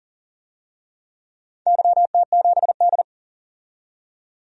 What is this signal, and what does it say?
Morse code, heard as a keyed tone, about 40 wpm, 700 Hz. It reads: YT7D
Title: DXCC-100 Call Signs - 4 Characters - 40wpm